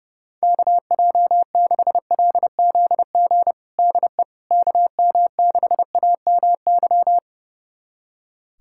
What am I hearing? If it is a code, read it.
Morse code, heard as a keyed tone, about 30 wpm, 700 Hz. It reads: KJ6LZG DE KM6AMY
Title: USA Callsign DE Callsign - 6 Characters - 30wpm